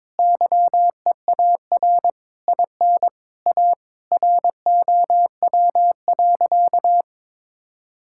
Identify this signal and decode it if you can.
Morse code, heard as a keyed tone, about 22 wpm, 700 Hz. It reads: YEAR IN A ROW.
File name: Sentences with 7-Word Phrases - 22wpm